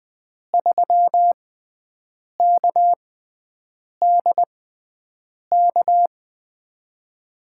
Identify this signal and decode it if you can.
Morse code, heard as a keyed tone, about 20 wpm, spaced as for 7 wpm, 700 Hz. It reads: 3KDK
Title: USA Call Signs - 5 Characters - 6x Character Spacing - 20wpm